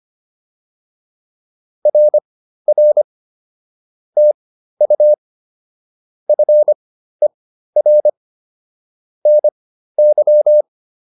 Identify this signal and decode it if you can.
Morse code, heard as a keyed tone, about 25 wpm, spaced as for 13 wpm, 600 Hz. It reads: RR TU FER NY